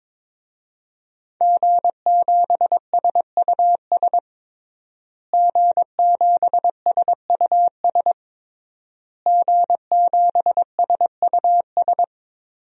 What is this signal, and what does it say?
Morse code, heard as a keyed tone, about 22 wpm, 700 Hz. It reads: G7SUS G7SUS G7SUS
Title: DXCC-100 Call Signs - 5 Characters - Repeated 3x - 22wpm